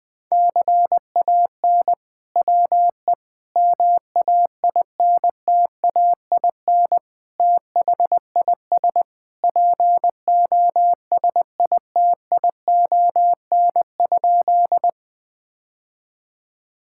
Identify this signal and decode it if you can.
Morse code, heard as a keyed tone, about 20 wpm, 700 Hz. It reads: CAN WE MAINTAIN THIS POSITION?
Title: Sentences from Top 1000 Words - Review Part 5 - 20wpm